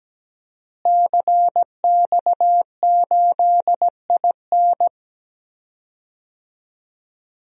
Morse code, heard as a keyed tone, about 17 wpm, 700 Hz. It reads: CX8IN
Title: DXCC-100 Call Signs - 5 Characters - Repeated 2x - 17wpm